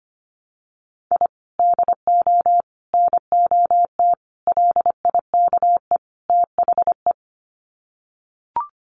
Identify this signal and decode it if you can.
Morse code, heard as a keyed tone, about 25 wpm, 700 Hz. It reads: I DO NOT LIKE THE
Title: Sets of 5 Words Encore 25wpm